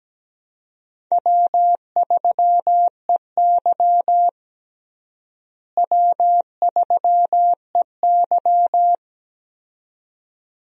Morse code, heard as a keyed tone, about 17 wpm, 700 Hz. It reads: W3EY W3EY
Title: USA Call Signs - 4 Characters - Repeated 2x - 17wpm